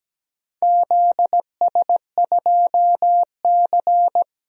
Morse code, heard as a keyed tone, about 17 wpm, 700 Hz. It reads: ZS2C
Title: DXCC-100 Call Signs - 4 Characters - 17wpm